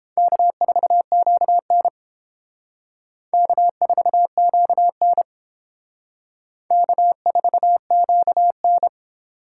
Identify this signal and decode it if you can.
Morse code, heard as a keyed tone, about 29 wpm, 700 Hz. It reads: K4QN K4QN K4QN